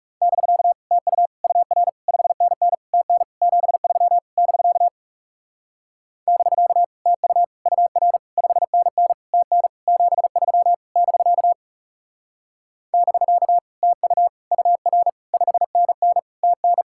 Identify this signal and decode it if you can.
Morse code, heard as a keyed tone, about 38 wpm, 700 Hz. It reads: <BK> TU UR 5NN TN 73 <BK> <BK> TU UR 5NN TN 73 <BK> <BK> TU UR 5NN TN